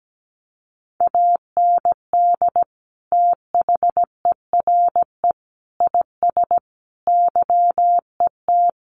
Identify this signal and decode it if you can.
Morse code, heard as a keyed tone, about 17 wpm, 700 Hz. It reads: AND THERE IS YET
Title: Sentences from Top 1500 Words 17wpm